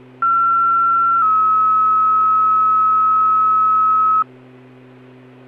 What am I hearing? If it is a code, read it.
Two-tone page, heard as a one-second tone, then a three-second tone, about 1360 then 1290 Hz. It is heard from 0.2 to 4.2 s.